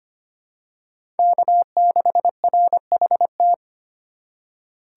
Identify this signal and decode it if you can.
Morse code, heard as a keyed tone, about 25 wpm, 700 Hz. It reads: K6RHT
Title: USA Call Signs - 5 Characters - 4x Character Spacing - 25wpm